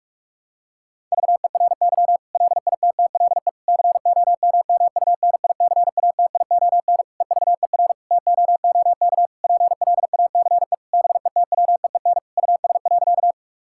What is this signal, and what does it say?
Morse code, heard as a keyed tone, about 46 wpm, 700 Hz. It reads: VERY LITTLE COMMUNICATION EVER TOOK PLACE BETWEEN US.
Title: Sentences from Top 1000 Words - Review Part 6 - 45wpm